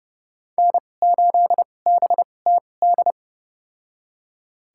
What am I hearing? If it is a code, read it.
Morse code, heard as a keyed tone, about 30 wpm, spaced as for 22 wpm, 700 Hz. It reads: N8BTD
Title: USA Call Signs - 2x Character Spacing - 30wpm